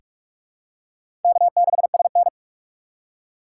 Morse code, heard as a keyed tone, about 46 wpm, 690 Hz. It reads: K6SN
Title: USA Call Signs - 4 Characters - 3x Character Spacing - 45wpm